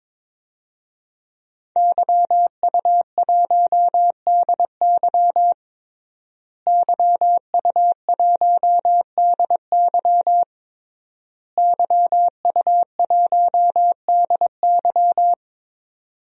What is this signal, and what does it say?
Morse code, heard as a keyed tone, about 22 wpm, 700 Hz. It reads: YU1DY YU1DY YU1DY
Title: DXCC-100 Call Signs - 5 Characters - Repeated 3x - 22wpm